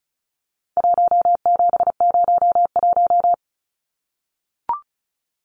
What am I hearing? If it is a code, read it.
Morse code, heard as a keyed tone, about 34 wpm, 700 Hz. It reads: 1701
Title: Numbers - 4-Digit 35wpm